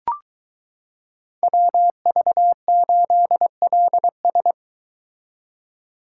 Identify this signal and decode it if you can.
Morse code, heard as a keyed tone, about 23 wpm, 700 Hz. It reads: WV8LS